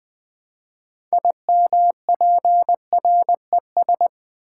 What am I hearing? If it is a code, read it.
Morse code, heard as a keyed tone, about 20 wpm, 700 Hz. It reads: IMPRES
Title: Build A Word - 3 Letters At A Time - 9-Letter Words - 20wpm